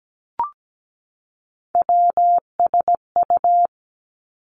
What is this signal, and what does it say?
Morse code, heard as a keyed tone, about 17 wpm, 700 Hz. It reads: WSU
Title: Three Challenging Letter-Number Combinations 17wpm